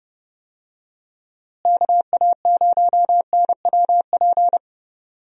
Morse code, heard as a keyed tone, about 30 wpm, 700 Hz. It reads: KA0NWP
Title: USA Call Signs - 6 Characters - Repeated 2x - 30wpm